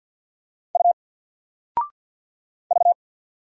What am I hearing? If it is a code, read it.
Morse code, heard as a keyed tone, about 50 wpm, 700 Hz. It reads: U V
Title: Single Letter-Number 50wpm